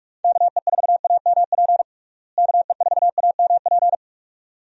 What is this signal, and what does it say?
Morse code, heard as a keyed tone, about 46 wpm, 700 Hz. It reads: KE4AMP KE4AMP
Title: USA Call Signs - 6 Characters - Repeated 2x - 45wpm